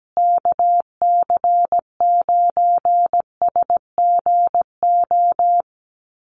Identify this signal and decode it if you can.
Morse code, heard as a keyed tone, about 17 wpm, 700 Hz. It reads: KC9SGO
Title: USA Call Signs Repeated 3x 17wpm